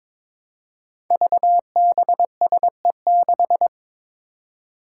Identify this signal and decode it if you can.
Morse code, heard as a keyed tone, about 22 wpm, 700 Hz. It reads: VBSE6
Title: Five Challenging Letter-Number Combinations 22wpm